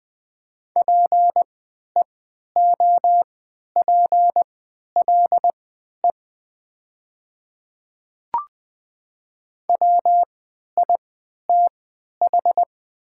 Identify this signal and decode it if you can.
Morse code, heard as a keyed tone, about 20 wpm, spaced as for 11 wpm, 700 Hz. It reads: PEOPLE WITH
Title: 3x Character Spacing - Top 100 Words - 20wpm